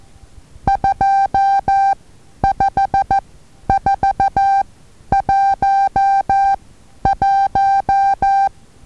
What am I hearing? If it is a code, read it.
Morse code, heard as a keyed tone, about 14 wpm, spaced as for 10 wpm, 790 Hz. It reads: 25411
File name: Taviro HA6VA A